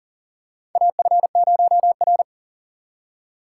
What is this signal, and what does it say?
Morse code, heard as a keyed tone, about 40 wpm, 700 Hz. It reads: AF0R
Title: USA Call Signs - 4 Characters - 3x Character Spacing - 40wpm